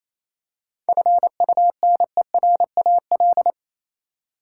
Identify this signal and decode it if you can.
Morse code, heard as a keyed tone, about 28 wpm, 700 Hz. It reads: FUNERAL
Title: Seven Letter Words - 28wpm